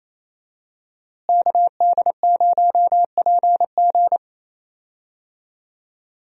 Morse code, heard as a keyed tone, about 28 wpm, 700 Hz. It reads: KD0PG
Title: USA Call Signs - 5 Characters - Repeated 3x - 28wpm